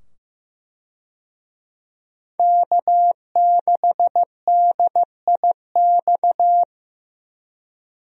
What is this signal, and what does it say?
Morse code, heard as a keyed tone, about 16 wpm, 700 Hz. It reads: K6DIX
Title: USA Call Signs - 5 Characters - 15wpm